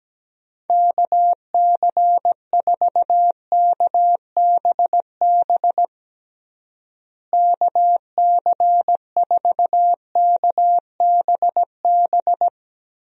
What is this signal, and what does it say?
Morse code, heard as a keyed tone, about 17 wpm, 700 Hz. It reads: KC4KBB KC4KBB
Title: USA Call Signs - 6 Characters - Repeated 3x - 17wpm